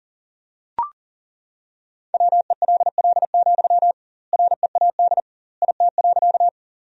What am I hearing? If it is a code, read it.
Morse code, heard as a keyed tone, about 40 wpm, 700 Hz. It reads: WELL, READ IT.
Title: 2x Word-Spacing - Sentences from Top 200 Words - 40wpm